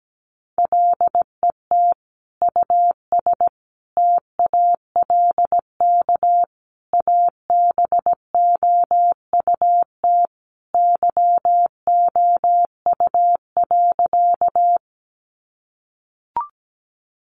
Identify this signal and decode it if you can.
Morse code, heard as a keyed tone, about 17 wpm, 700 Hz. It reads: LET US TALK ABOUT YOU.
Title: Sentences from Top 400 Words 17wpm